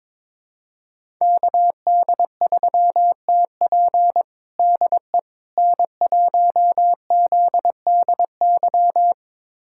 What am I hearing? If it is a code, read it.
Morse code, heard as a keyed tone, about 22 wpm, 700 Hz. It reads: KD3TP DE N1ZDY
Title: USA Callsign DE Callsign - 5 Characters - 22wpm